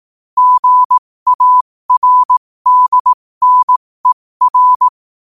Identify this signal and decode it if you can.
Morse code, heard as a keyed tone, about 18 wpm, spaced as for 15 wpm, 1000 Hz. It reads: GARDNER